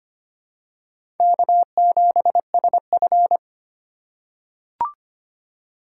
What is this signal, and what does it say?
Morse code, heard as a keyed tone, about 25 wpm, 700 Hz. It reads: K7SF
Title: K1USN SST QSO Practice - 25wpm